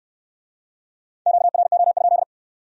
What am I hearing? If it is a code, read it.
Morse code, heard as a keyed tone, about 68 wpm, 700 Hz. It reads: QRO?